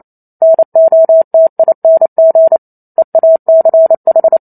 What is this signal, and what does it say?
Morse code, heard as a keyed tone, about 28 wpm, 650 Hz. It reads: NOTING EACH